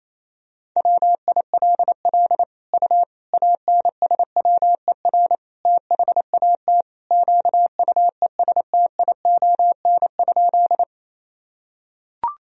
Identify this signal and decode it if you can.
Morse code, heard as a keyed tone, about 28 wpm, 700 Hz. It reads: WILL U ANSWER THAT QUESTION?
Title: Sentences from Top 400 Words and Abbreviations 28wpm